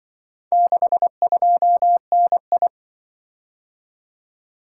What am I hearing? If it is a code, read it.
Morse code, heard as a keyed tone, about 25 wpm, 700 Hz. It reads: 62NI
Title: Speed-Racing - Four Letter-Number Combinations - 20wpm